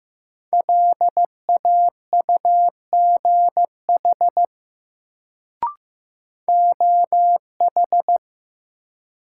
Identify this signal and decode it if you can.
Morse code, heard as a keyed tone, about 15 wpm, 700 Hz. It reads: LAUGH OH
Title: Sentences from Top 500 Words - New Word Drill - 15wpm